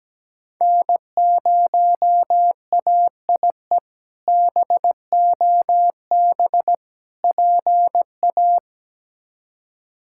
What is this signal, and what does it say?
Morse code, heard as a keyed tone, about 17 wpm, 700 Hz. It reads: N0AIE BOB PA
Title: K1USN SST QSO Practice - 17wpm